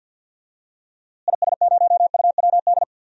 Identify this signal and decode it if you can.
Morse code, heard as a keyed tone, about 46 wpm, 700 Hz. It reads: IS0UWD
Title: DXCC-100 Call Signs - 6 Characters - Repeated 2x - 50wpm